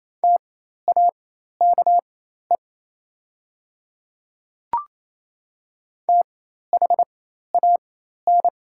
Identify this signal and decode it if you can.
Morse code, heard as a keyed tone, about 28 wpm, spaced as for 13 wpm, 700 Hz. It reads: TAKE THAN
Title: 4x Character Spacing - Top 500 Words - 28wpm